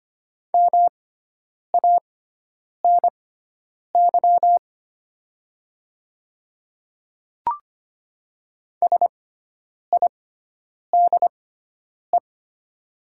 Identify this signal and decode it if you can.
Morse code, heard as a keyed tone, about 25 wpm, spaced as for 9 wpm, 700 Hz. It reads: MANY SIDE
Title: 6x Character Spacing - Top 100 Words - 25wpm